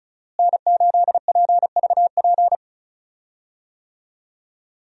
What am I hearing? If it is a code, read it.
Morse code, heard as a keyed tone, about 36 wpm, 700 Hz. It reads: N8PVP
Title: USA Call Signs - 5 Characters - Repeated 3x - 35wpm